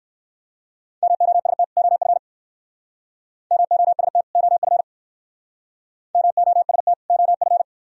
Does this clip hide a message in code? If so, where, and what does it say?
Yes: Morse code from 1.0 to 7.6 s, heard as a keyed tone, about 57 wpm, 700 Hz. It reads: MOST OF MOST OF MOST OF